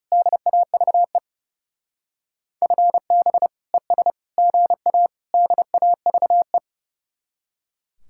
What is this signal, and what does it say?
Morse code, heard as a keyed tone, about 35 wpm, 700 Hz. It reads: DAVE FB ES GA DAVE
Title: Speed-Racing Common QSO Phrases - 30wpm